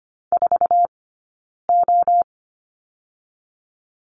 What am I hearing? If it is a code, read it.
Morse code, heard as a keyed tone, about 25 wpm, 700 Hz. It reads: 4 O
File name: Lesson 17d - Letter u - Two Characters 25wpm